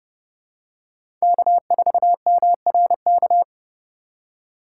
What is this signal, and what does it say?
Morse code, heard as a keyed tone, about 30 wpm, 700 Hz. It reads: K4MRK
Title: USA Call Signs - 5 Characters - 2x Character Spacing - 30wpm